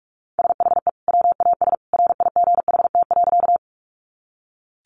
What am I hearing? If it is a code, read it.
Morse code, heard as a keyed tone, about 44 wpm, 700 Hz. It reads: SHE WAS RIGHT.
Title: Sentences from Top 200 Words 45wpm